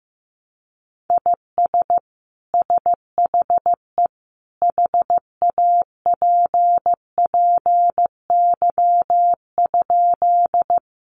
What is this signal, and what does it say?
Morse code, heard as a keyed tone, about 15 wpm, 700 Hz. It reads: IS SHE HAPPY?
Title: Sentences from Top 1000 Words 15wpm